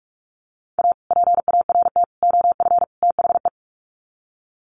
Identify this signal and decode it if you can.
Morse code, heard as a keyed tone, about 44 wpm, 700 Hz. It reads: A PART OF THE
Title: Sets of 4 Words 45wpm